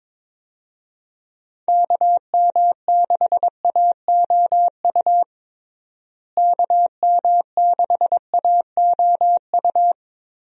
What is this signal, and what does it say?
Morse code, heard as a keyed tone, about 22 wpm, 700 Hz. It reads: KM6AOU KM6AOU
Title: USA Call Signs - 6 Characters - Repeated 3x - 22wpm